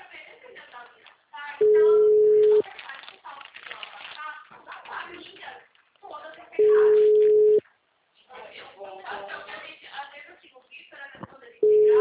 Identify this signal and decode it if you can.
Ringback tone (Europe, 425 Hz), with speech.
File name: out- -2033- - - .71451